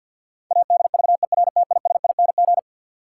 Morse code, heard as a keyed tone, about 47 wpm, 700 Hz. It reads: ADVERTISING
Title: Sentences from Top 1000 Words - Review Part 6 - New Word Drill - 50wpm